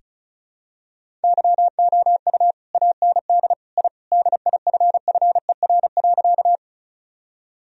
Morse code, heard as a keyed tone, about 34 wpm, 700 Hz. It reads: YOU AND I DIFFER.
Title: Sentences from Top 1000 Words - Review Part 1 - 35wpm